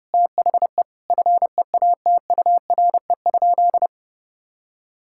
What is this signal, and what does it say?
Morse code, heard as a keyed tone, about 32 wpm, 700 Hz. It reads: THE FEATURE?
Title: Sentences from Top 1000 Words - Review Part 5 - 30wpm